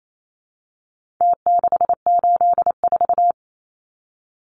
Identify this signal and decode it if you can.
Morse code, heard as a keyed tone, about 28 wpm, 700 Hz. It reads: T684
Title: Numbers - Contest Style - 4-Digit 28wpm